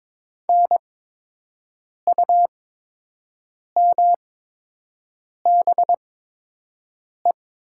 Morse code, heard as a keyed tone, about 22 wpm, spaced as for 6 wpm, 700 Hz. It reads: NUMBE
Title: 8x Character Spacing - Top 200 Words - 22wpm